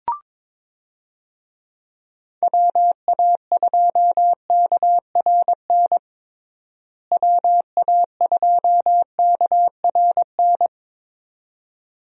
Morse code, heard as a keyed tone, about 22 wpm, 700 Hz. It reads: WA2KRN WA2KRN
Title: USA Call Signs - 6 Characters - Repeated 2x - 22wpm